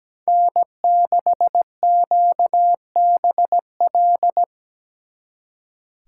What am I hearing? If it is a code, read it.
Morse code, heard as a keyed tone, about 20 wpm, 700 Hz. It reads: N6QBL
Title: Speed Racing - USA 1x3 Call Signs 17wpm